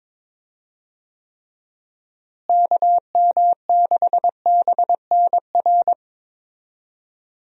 Morse code, heard as a keyed tone, about 22 wpm, 700 Hz. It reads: KM6BNR
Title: USA Call Signs - 6 Characters - Repeated 3x - 22wpm